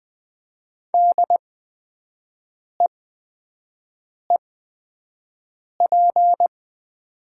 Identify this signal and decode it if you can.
Morse code, heard as a keyed tone, about 20 wpm, spaced as for 5 wpm, 700 Hz. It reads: DEEP
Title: 8x Character Spacing - Top 500 Words - 20wpm